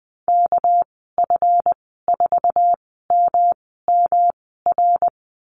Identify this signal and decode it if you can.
Morse code, heard as a keyed tone, about 20 wpm, spaced as for 15 wpm, 700 Hz. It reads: KF4MMR
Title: 207 Lesson 33g - Letter j - Call Signs 20wpm